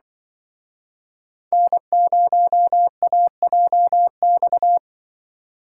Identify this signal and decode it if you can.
Morse code, heard as a keyed tone, about 25 wpm, 700 Hz. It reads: N0AJX